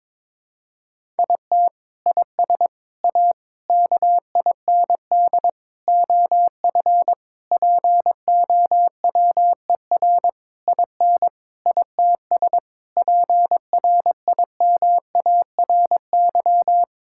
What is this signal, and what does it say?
Morse code, heard as a keyed tone, about 22 wpm, 700 Hz. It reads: IT IS A KIND OF POWER IN ITS PRIMARY
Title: Sentences from Top 1000 Words - Review Part 5 - 22wpm